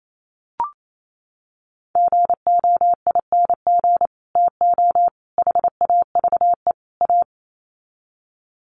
Morse code, heard as a keyed tone, about 28 wpm, 700 Hz. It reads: GOING TO HAVE A
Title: Sets of 4 Words 28wpm